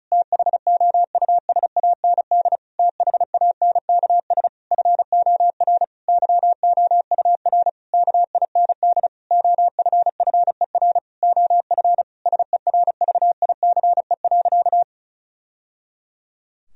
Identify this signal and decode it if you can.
Morse code, heard as a keyed tone, about 34 wpm, 700 Hz. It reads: THOUSAND THANKS FOR YOUR KIND OFFER OF SERVICE.
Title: Sentences from Top 1000 Words - Review Part 4 - 35wpm